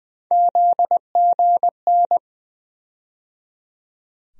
Morse code, heard as a keyed tone, about 19 wpm, 700 Hz. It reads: ZGN
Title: USA Call Sign Postfixes - 3 Letters 20wpm